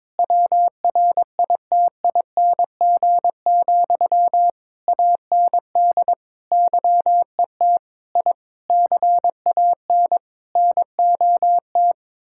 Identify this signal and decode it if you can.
Morse code, heard as a keyed tone, about 22 wpm, 700 Hz. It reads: WRITING, AND YET I CAN NOT
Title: Sentences from Top 1000 Words - Review Part 6 - 22wpm